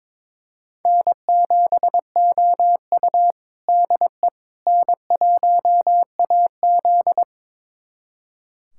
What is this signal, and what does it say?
Morse code, heard as a keyed tone, about 22 wpm, 700 Hz. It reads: N7OU DE N1AZ
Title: USA Callsign DE Callsign - 4 Characters - 22wpm